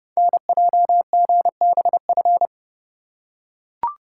Morse code, heard as a keyed tone, about 30 wpm, 700 Hz. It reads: NJGBF
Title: Code Groups - 5 Letters - 30wpm